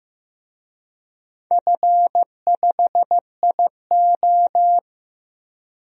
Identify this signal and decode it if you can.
Morse code, heard as a keyed tone, about 15 wpm, 700 Hz. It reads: F5IO
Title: DXCC-100 Call Signs - 4 Characters - Repeated 3x - 15wpm